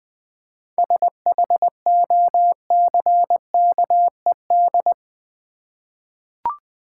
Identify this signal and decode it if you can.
Morse code, heard as a keyed tone, about 20 wpm, 700 Hz. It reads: SHOCKED
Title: 7 Letter Words - First 3 Letters Given - 20wpm